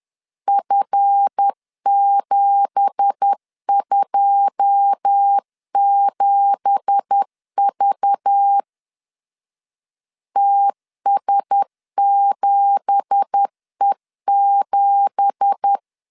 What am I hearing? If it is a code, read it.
Morse code, heard as a keyed tone, about 10 wpm, 800 Hz. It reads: F727V TS7E7
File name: LEZIONI / 9 2867 10wpm 2733